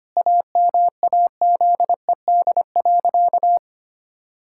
Morse code, heard as a keyed tone, about 25 wpm, 700 Hz. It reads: AMAZED.
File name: practice / 25wpm / Sentences with 4-Word Phrases - 25wpm